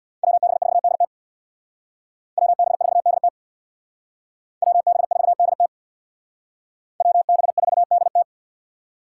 Morse code, heard as a keyed tone, about 62 wpm, 700 Hz. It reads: WB4DT WB4DT WB4DT WB4DT
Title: Speed Racing - USA 2x2 Call Signs 50wpm